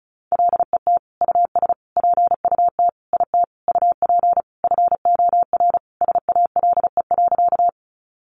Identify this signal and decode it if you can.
Morse code, heard as a keyed tone, about 34 wpm, 700 Hz. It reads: LET US PUT IT UP FOR SALE.
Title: Sentences from Top 1400 Words 35wpm